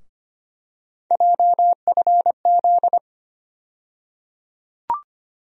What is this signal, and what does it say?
Morse code, heard as a keyed tone, about 25 wpm, 700 Hz. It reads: JFZ
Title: USA Call Sign Postfixes - 3 Letters 25wpm